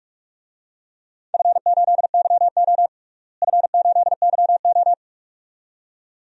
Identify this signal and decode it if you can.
Morse code, heard as a keyed tone, about 46 wpm, 700 Hz. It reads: F8YO F8YO
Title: DXCC-100 Call Signs - 4 Characters - Repeated 2x - 45wpm